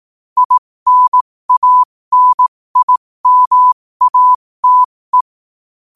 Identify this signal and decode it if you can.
Morse code, heard as a keyed tone, about 18 wpm, spaced as for 15 wpm, 1000 Hz. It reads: INANIMATE
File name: qotdfiles / 15wpm